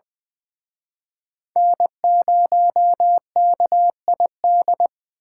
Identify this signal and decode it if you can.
Morse code, heard as a keyed tone, about 24 wpm, 700 Hz. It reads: N0KID
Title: Speed Racing - USA Call Signs 20wpm